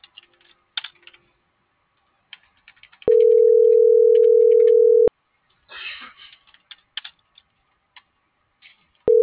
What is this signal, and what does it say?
Ringback tone (North America), 440 + 480 Hz.